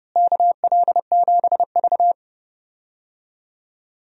Morse code, heard as a keyed tone, about 32 wpm, 700 Hz. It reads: KL7V
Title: DXCC-100 Call Signs - 4 Characters - Repeated 2x - 30wpm